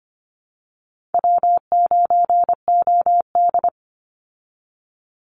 Morse code, heard as a keyed tone, about 25 wpm, 700 Hz. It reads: W9OD
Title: Call Signs Repeated 2x - 25wpm